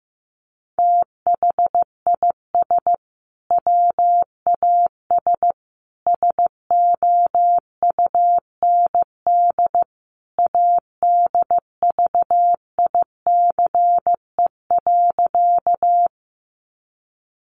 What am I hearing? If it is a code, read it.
Morse code, heard as a keyed tone, about 15 wpm, 700 Hz. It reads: THIS WAS SOUND ADVICE.
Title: Sentences from Top 1100 Words 15wpm